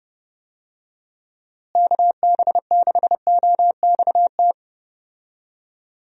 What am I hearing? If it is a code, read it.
Morse code, heard as a keyed tone, about 31 wpm, 700 Hz. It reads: KB6OXT
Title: USA Call Signs - 6 Characters - Repeated 2x - 30wpm